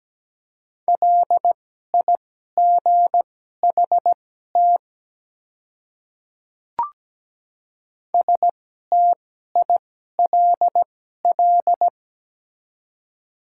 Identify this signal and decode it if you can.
Morse code, heard as a keyed tone, about 17 wpm, spaced as for 12 wpm, 700 Hz. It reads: LIGHT STILL
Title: 2x Character Spacing - Top 300 Words - 17wpm